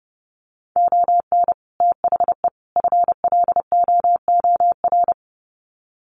Morse code, heard as a keyed tone, about 30 wpm, 700 Hz. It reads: ON THE FLOOR
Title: Sets of 3 Words 30wpm